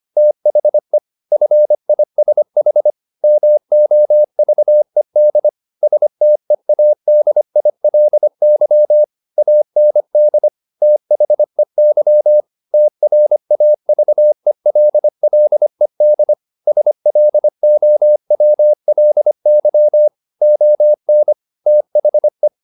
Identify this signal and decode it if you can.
Morse code, heard as a keyed tone, about 25 wpm, 600 Hz. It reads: THE FISH MOVED STEADILY AND THEY TRAVELLED SLOWLY ON THE
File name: the old man and the sea part 1 25wpm